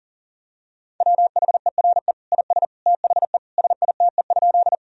Morse code, heard as a keyed tone, about 40 wpm, 700 Hz. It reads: WHERE IS THE SITE?
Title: Sentences from Top 1000 Words - Review Part 4 - 40wpm